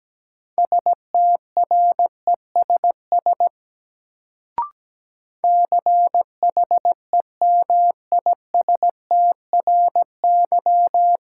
Morse code, heard as a keyed tone, about 17 wpm, 700 Hz. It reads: STRESS CHEMISTRY
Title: Sentences from Top 1000 Words - Review Part 6 - New Word Drill - 17wpm